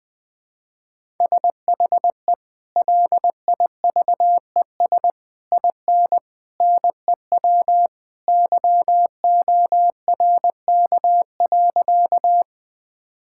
Morse code, heard as a keyed tone, about 21 wpm, 700 Hz. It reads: SHE LIVES IN NEW YORK.
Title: Sentences with 3-Word Phrases - 20wpm